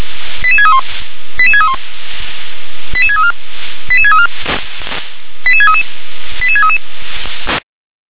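Five-tone selective call (ZVEI), heard as quick runs of five tones, tones of about 80 ms. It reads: 88631 88631 88634 88634 88633 88633
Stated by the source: ZVEI five-tone selective call received on SDR.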